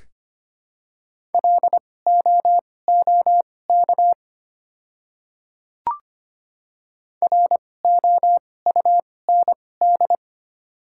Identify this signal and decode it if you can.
Morse code, heard as a keyed tone, about 25 wpm, spaced as for 18 wpm, 700 Hz. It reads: LOOK ROUND
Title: 2x Character Spacing - Top 300 Words - 25wpm